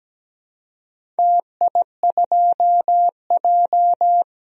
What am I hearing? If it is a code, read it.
Morse code, heard as a keyed tone, about 17 wpm, 700 Hz. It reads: TI2J
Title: DXCC-100 Call Signs - 4 Characters - 17wpm